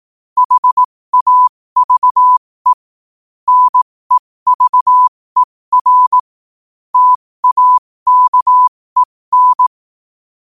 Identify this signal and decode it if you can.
Morse code, heard as a keyed tone, about 18 wpm, spaced as for 15 wpm, 1000 Hz. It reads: HAVE NEVER TAKEN
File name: qotdfiles / 15wpm